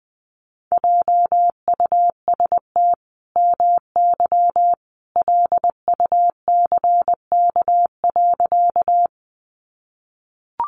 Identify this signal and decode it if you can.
Morse code, heard as a keyed tone, about 20 wpm, 700 Hz. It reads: JUST MY LUCK.
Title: Sentences from Top 1500 Words 20wpm